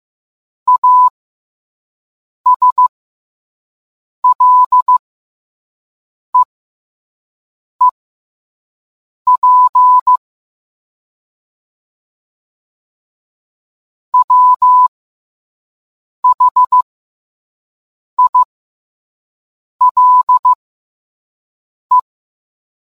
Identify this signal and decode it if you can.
Morse code, heard as a keyed tone, about 15 wpm, spaced as for 5 wpm, 1000 Hz. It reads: ASLEEP WHILE